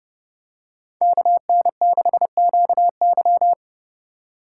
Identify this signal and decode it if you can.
Morse code, heard as a keyed tone, about 30 wpm, 700 Hz. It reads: KN6QY